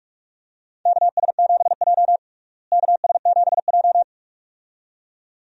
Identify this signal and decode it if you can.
Morse code, heard as a keyed tone, about 46 wpm, 700 Hz. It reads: KS7J KS7J
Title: USA Call Signs - 4 Characters - Repeated 2x - 45wpm